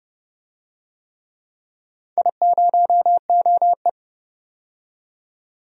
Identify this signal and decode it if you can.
Morse code, heard as a keyed tone, about 30 wpm, 700 Hz. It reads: I0OE
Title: DXCC-100 Call Signs - 4 Characters - 30wpm